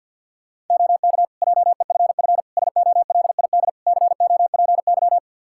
Morse code, heard as a keyed tone, about 46 wpm, 700 Hz. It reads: OK JEFF SOLID COPY
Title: K1USN SST QSO Practice 50wpm